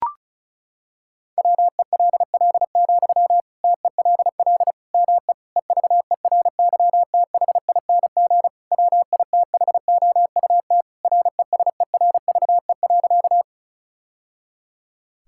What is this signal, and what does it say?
Morse code, heard as a keyed tone, about 36 wpm, 700 Hz. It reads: WELL, TELL ME EVERYTHING WITHOUT RESERVE.
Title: Sentences from Top 1000 Words - Review Part 6 - 35wpm